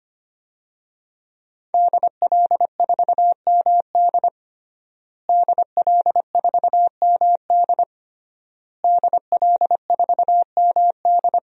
Morse code, heard as a keyed tone, about 25 wpm, 700 Hz. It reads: DL4MD DL4MD DL4MD
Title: DXCC-100 Call Signs - 5 Characters - Repeated 3x - 25wpm